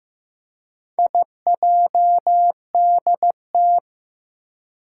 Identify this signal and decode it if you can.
Morse code, heard as a keyed tone, about 15 wpm, 700 Hz. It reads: IJDT
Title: Four Challenging Letter-Number Combinations 15wpm